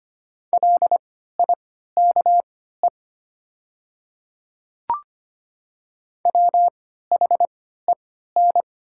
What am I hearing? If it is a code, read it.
Morse code, heard as a keyed tone, about 25 wpm, spaced as for 14 wpm, 700 Hz. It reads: LIKE WHEN
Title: 3x Character Spacing - Top 100 Words - 25wpm